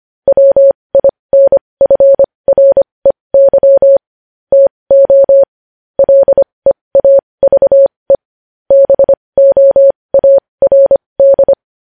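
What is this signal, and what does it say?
Morse code, heard as a keyed tone, about 25 wpm, spaced as for 20 wpm, 550 Hz. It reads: WINFREY TO LEAVE BOARD
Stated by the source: News headlines in Morse at 20 wpm.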